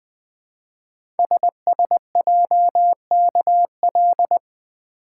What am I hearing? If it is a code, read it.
Morse code, heard as a keyed tone, about 20 wpm, 700 Hz. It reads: SSJKL
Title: Code Groups - 5 Letters - 20wpm